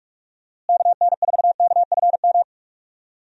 Morse code, heard as a keyed tone, about 44 wpm, 700 Hz. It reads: KN4KFM
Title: USA Call Signs - 6 Characters - Repeated 3x - 45wpm